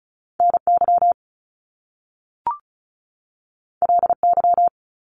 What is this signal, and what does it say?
Morse code, heard as a keyed tone, about 36 wpm, 700 Hz. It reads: NY LY
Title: Top Two Letter Combinations in Words 35wpm